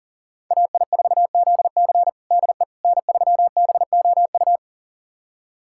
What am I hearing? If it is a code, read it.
Morse code, heard as a keyed tone, about 37 wpm, 700 Hz. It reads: AI4ZC DE N3BOU
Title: USA Callsign DE Callsign - 5 Characters - 40wpm